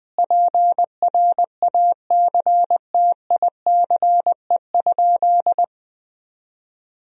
Morse code, heard as a keyed tone, about 20 wpm, 700 Hz. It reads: PRACTICE?
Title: Sentences from Top 1000 Words - Review Part 4 - 20wpm